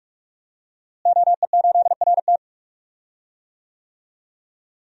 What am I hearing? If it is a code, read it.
Morse code, heard as a keyed tone, about 41 wpm, 700 Hz. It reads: OE8RT
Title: DXCC-100 Call Signs - 5 Characters - Repeated 3x - 45wpm